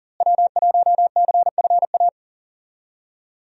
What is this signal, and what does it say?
Morse code, heard as a keyed tone, about 40 wpm, 700 Hz. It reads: W1CFA
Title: USA Call Signs - 5 Characters - Repeated 3x - 40wpm